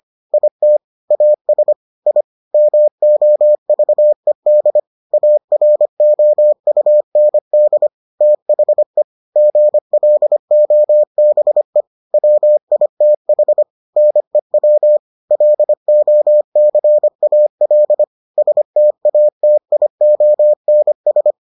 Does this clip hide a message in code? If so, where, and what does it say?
Yes: Morse code from 0.3 to 21.3 s, heard as a keyed tone, about 25 wpm, 600 Hz. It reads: IT AS I MOVED AROUND THE GLOBE WITH NEW LOCAL STATIONS